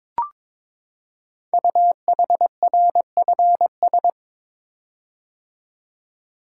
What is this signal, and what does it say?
Morse code, heard as a keyed tone, about 22 wpm, 700 Hz. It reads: UHRFS
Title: Code Groups - 5 Letters - 22wpm